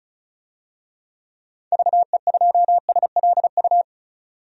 Morse code, heard as a keyed tone, about 36 wpm, 700 Hz. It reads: VE2SLU
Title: DXCC-100 Call Signs - 6 Characters - 35wpm